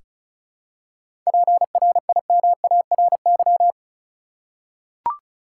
Morse code, heard as a keyed tone, about 36 wpm, 700 Hz. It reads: PRIMARY
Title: Seven Letter Words - 35wpm